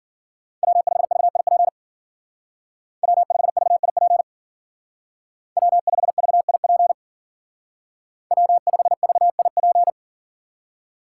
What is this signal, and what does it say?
Morse code, heard as a keyed tone, about 52 wpm, 700 Hz. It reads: W5VIP W5VIP W5VIP W5VIP